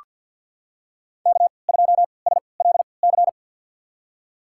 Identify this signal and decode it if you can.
Morse code, heard as a keyed tone, about 47 wpm, spaced as for 27 wpm, 700 Hz. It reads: K2SLC
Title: USA Call Signs - 5 Characters - 3x Character Spacing - 50wpm